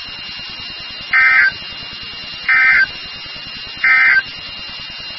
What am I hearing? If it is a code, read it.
Emergency Alert System (SAME) tones, heard as harsh buzzing bursts. It is heard from 1.0 to 4.3 s.
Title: NOAA Weather Radio WXK36